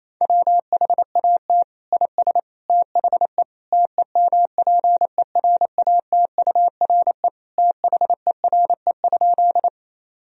Morse code, heard as a keyed tone, about 28 wpm, 700 Hz. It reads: WHAT IS THE TEMPERATURE THERE?
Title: Sentences from Top 1000 Words - Review Part 4 - 28wpm